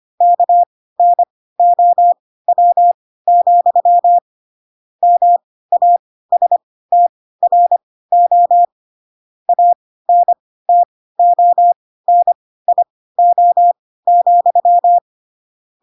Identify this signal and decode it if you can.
Morse code, heard as a keyed tone, about 25 wpm, spaced as for 16 wpm, 700 Hz. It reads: KNOW, MASTRO ANTONIO,
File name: Pinocchio / Pinocchio ch2 10 25 ik1hge